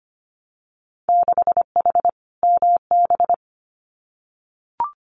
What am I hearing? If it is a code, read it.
Morse code, heard as a keyed tone, about 25 wpm, 700 Hz. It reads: 6H MB
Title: Winter Field Day Exchanges Encore 25wpm